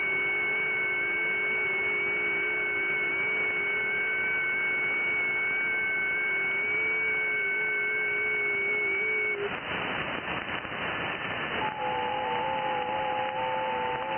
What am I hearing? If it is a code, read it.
FT8.
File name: IC-7100